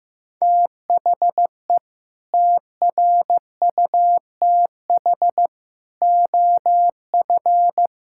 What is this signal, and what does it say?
Morse code, heard as a keyed tone, about 16 wpm, 700 Hz. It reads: THE TRUTH OF
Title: Sentences from Top 1600 Words 15wpm